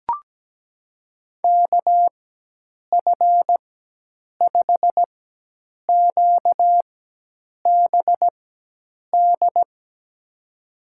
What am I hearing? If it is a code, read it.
Morse code, heard as a keyed tone, about 17 wpm, spaced as for 8 wpm, 700 Hz. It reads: KF5QBD
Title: USA Call Signs - 6 Characters - 4x Character Spacing - 17wpm